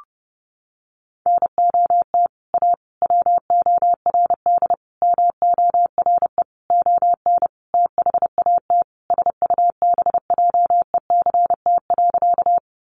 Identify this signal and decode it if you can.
Morse code, heard as a keyed tone, about 30 wpm, 700 Hz. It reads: NOT A WORD MORE ON THAT SUBJECT.
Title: Sentences from Top 800 Words 30wpm